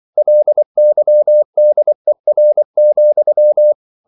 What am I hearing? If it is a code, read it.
Morse code, heard as a keyed tone, about 24 wpm, 600 Hz. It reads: LYDER,
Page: Ferie 23 wpm